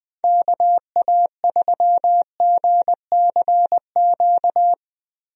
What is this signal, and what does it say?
Morse code, heard as a keyed tone, about 21 wpm, 700 Hz. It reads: KA3GCQ
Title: USA Call Signs - 6 Characters - Repeated 3x - 20wpm